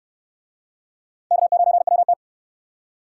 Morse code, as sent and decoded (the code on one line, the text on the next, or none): -.- ----. .-- -
K9WT